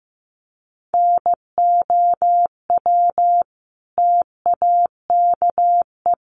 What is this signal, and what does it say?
Morse code, heard as a keyed tone, about 15 wpm, 700 Hz. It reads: NOW TAKE
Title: Sentences from Top 1100 Words 15wpm